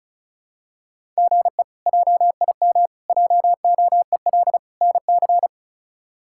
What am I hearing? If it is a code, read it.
Morse code, heard as a keyed tone, about 36 wpm, 700 Hz. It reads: GE JIM JOEL NC
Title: K1USN SST QSO Practice - 35wpm